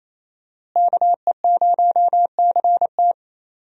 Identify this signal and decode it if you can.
Morse code, heard as a keyed tone, about 27 wpm, 700 Hz. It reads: KE0CT
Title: USA Call Signs - 5 Characters - Repeated 3x - 28wpm